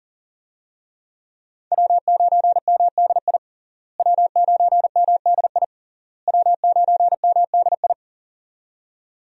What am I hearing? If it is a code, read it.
Morse code, heard as a keyed tone, about 40 wpm, 700 Hz. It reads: W9MDI W9MDI W9MDI
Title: USA Call Signs - 5 Characters - Repeated 3x - 40wpm